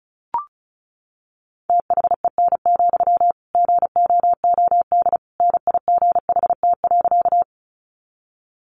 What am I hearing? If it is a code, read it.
Morse code, heard as a keyed tone, about 34 wpm, 700 Hz. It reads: THEN, GOOD NIGHT.
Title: Sentences from Top 300 Words 35wpm